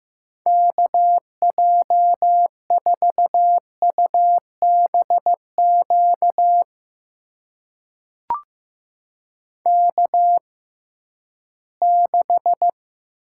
Morse code, heard as a keyed tone, about 15 wpm, spaced as for 11 wpm, 700 Hz. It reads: KJ4UBQ K6
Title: USA Call Signs - 6x Character Spacing - 15wpm